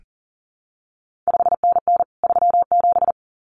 Morse code, heard as a keyed tone, about 41 wpm, 700 Hz. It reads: 5NN 37
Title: CQ WW DX Contest Exchange Encore - 40wpm